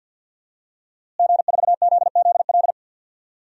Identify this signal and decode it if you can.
Morse code, heard as a keyed tone, about 49 wpm, 700 Hz. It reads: G4ZZL
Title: DXCC-100 Call Signs - 5 Characters - Repeated 3x - 50wpm